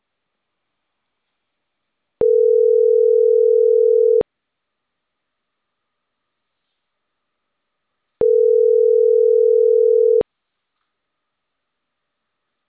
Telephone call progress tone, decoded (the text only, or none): ringback tone